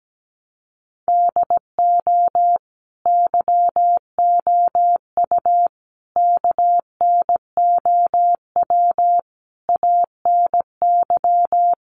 Morse code, heard as a keyed tone, about 17 wpm, 700 Hz. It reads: DO YOU KNOW ANY
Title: Sentences from Top 1200 Words 17wpm